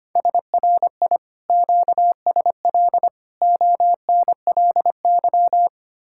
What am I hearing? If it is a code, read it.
Morse code, heard as a keyed tone, about 30 wpm, 700 Hz. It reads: SRI QSL ONLY
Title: Speed-Racing Common QSO Phrases - 25wpm